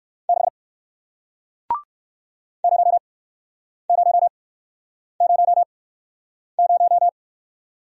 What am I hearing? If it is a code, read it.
Morse code, heard as a keyed tone, about 58 wpm, spaced as for 9 wpm, 700 Hz. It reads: 6 0000